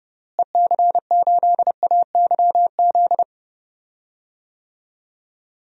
Morse code, heard as a keyed tone, about 30 wpm, 700 Hz. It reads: EC8AYZ
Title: DXCC-100 Call Signs - 6 Characters - Repeated 3x - 30wpm